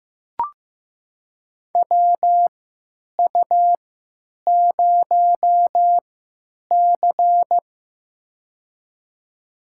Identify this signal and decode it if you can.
Morse code, heard as a keyed tone, about 16 wpm, spaced as for 9 wpm, 700 Hz. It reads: WU0C
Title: USA Call Signs - 3x Character Spacing - 15wpm